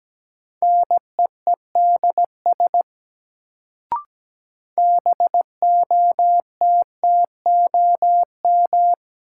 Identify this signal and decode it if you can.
Morse code, heard as a keyed tone, about 17 wpm, 700 Hz. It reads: NEEDS BOTTOM
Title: Sets of 3 Words - New Word Drill - 17wpm